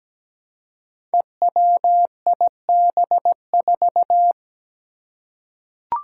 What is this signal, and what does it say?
Morse code, heard as a keyed tone, about 17 wpm, 700 Hz. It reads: EWIB4
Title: Five Challenging Letter-Number Combinations 17wpm